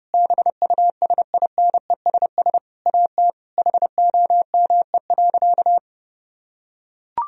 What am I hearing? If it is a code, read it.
Morse code, heard as a keyed tone, about 32 wpm, 700 Hz. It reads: BUSINESS AT HOME.
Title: Sentences from Top 1000 Words - Review Part 4 - 30wpm